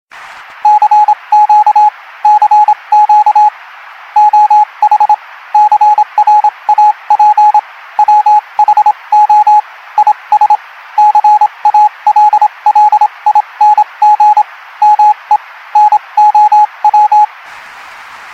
Morse code, heard as a keyed tone, about 27 wpm, spaced as for 23 wpm, 840 Hz, with radio noise. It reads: CQ CQ OH CRAP WHO IS CALLING ME NOW
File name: Morse code cloud - Morse radio Ringtone